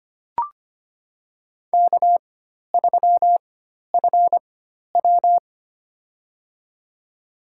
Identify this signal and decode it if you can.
Morse code, heard as a keyed tone, about 25 wpm, spaced as for 12 wpm, 700 Hz. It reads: K3FW
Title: USA Call Signs - 4 Characters - 4x Character Spacing - 25wpm